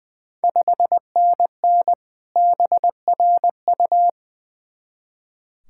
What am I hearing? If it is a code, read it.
Morse code, heard as a keyed tone, about 21 wpm, 700 Hz. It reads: 5NN BRU
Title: Ontario QSO Party Exchange 20wpm